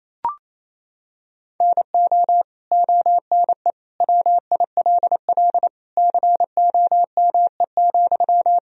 Morse code, heard as a keyed tone, about 28 wpm, 700 Hz. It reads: NO ONE WILL COME,
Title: Sentences from Top 100 Words and Abbreviations 28wpm